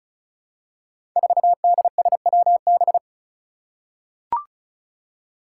Morse code, heard as a keyed tone, about 36 wpm, 700 Hz. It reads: 4DSWB
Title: Five Challenging Letter-Number Combinations 35wpm